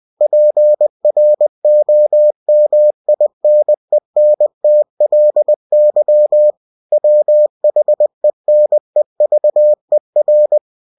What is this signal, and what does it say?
Morse code, heard as a keyed tone, about 20 wpm, 600 Hz. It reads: PROMINENTLY WHENEVER
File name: cw-books / winnie-the-pooh / winnie-20wpm-page-0115